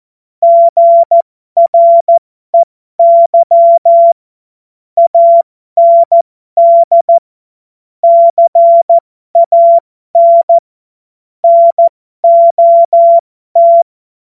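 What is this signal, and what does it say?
Morse code, heard as a keyed tone, about 14 wpm, spaced as for 12 wpm, 680 Hz. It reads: GREY AND CAN NOT
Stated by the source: Morse code joke practice.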